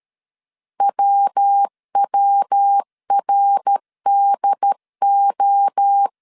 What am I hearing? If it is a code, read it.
Morse code, heard as a keyed tone, about 13 wpm, 800 Hz. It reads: WWRDO
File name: CW / DET / 1 3 1500C S B 12wpm 2828